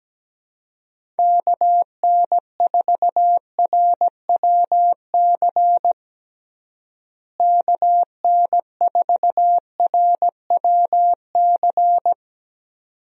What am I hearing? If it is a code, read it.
Morse code, heard as a keyed tone, about 17 wpm, 700 Hz. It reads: KN4RWC KN4RWC
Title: USA Call Signs - 6 Characters - Repeated 2x - 17wpm